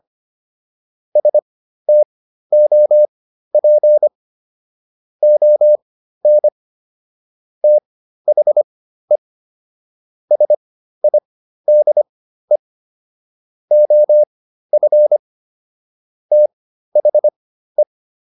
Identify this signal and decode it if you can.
Morse code, heard as a keyed tone, about 25 wpm, spaced as for 13 wpm, 600 Hz. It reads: STOP ON THE SIDE OF THE